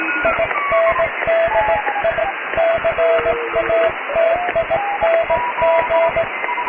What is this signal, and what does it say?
Morse code, heard as a keyed tone, about 17 wpm, 660 Hz, with radio noise. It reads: INDICATING
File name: cw / tekst(b)+QRM